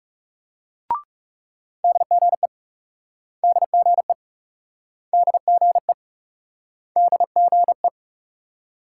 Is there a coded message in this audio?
Yes: Morse code from 1.8 to 7.9 s, heard as a keyed tone, about 38 wpm, 700 Hz. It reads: DGE DGE DGE DGE